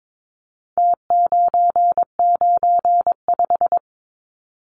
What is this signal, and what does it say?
Morse code, heard as a keyed tone, about 22 wpm, 700 Hz. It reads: T995
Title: Numbers - Contest Style - 4-Digit 22wpm